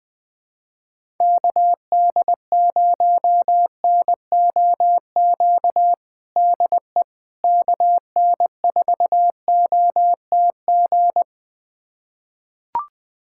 Morse code, heard as a keyed tone, about 20 wpm, 700 Hz. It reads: KD0NOQ DE KN4OTG
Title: USA Callsign DE Callsign - 6 Characters - 20wpm